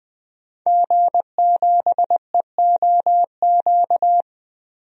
Morse code, heard as a keyed tone, about 21 wpm, 700 Hz. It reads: G7EOQ
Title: DXCC-100 Call Signs - 5 Characters - Repeated 2x - 20wpm